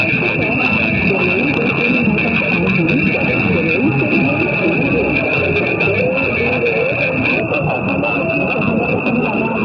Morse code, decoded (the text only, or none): none